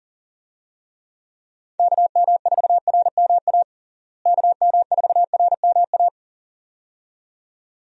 Morse code, heard as a keyed tone, about 40 wpm, 700 Hz. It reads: KM4RMA KM4RMA
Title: USA Call Signs - 6 Characters - Repeated 2x - 40wpm